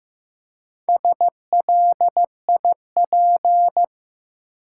Morse code, heard as a keyed tone, about 15 wpm, 700 Hz. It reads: SLIP
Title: Four Letter Words - 15wpm